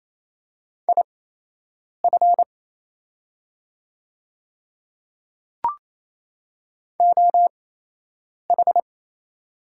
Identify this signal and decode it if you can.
Morse code, heard as a keyed tone, about 28 wpm, spaced as for 8 wpm, 690 Hz. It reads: IF OH
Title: 8x Character Spacing - Top 500 Words - 28wpm